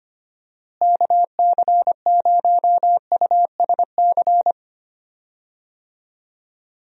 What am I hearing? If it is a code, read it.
Morse code, heard as a keyed tone, about 25 wpm, 700 Hz. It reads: KC0USC
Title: USA Call Signs - 6 Characters - Repeated 3x - 25wpm